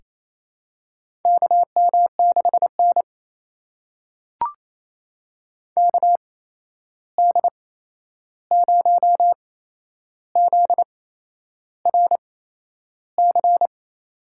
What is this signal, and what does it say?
Morse code, heard as a keyed tone, about 28 wpm, spaced as for 11 wpm, 700 Hz. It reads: KM6N KD0ZRC
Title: USA Call Signs - 8x Character Spacing - 28wpm